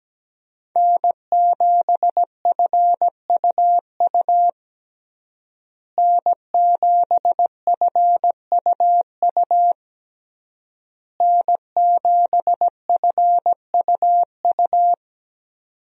Morse code, heard as a keyed tone, about 17 wpm, 700 Hz. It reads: N7FUU N7FUU N7FUU
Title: USA Call Signs - 5 Characters - Repeated 3x - 17wpm